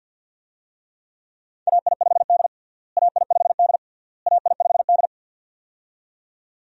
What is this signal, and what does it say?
Morse code, heard as a keyed tone, about 52 wpm, 700 Hz. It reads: AI5D AI5D AI5D
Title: USA Call Signs - 4 Characters - Repeated 3x - 50wpm